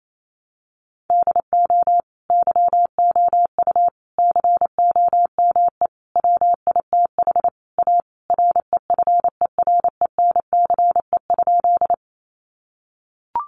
Morse code, heard as a keyed tone, about 28 wpm, 700 Hz. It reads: DO YOU COME WITH A REFERENCE?
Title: Sentences from Top 1200 Words 28wpm